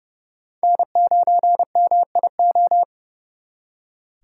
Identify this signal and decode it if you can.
Morse code, heard as a keyed tone, about 37 wpm, 700 Hz. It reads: N9MIO
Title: Speed Racing - USA 1x3 Call Signs 30wpm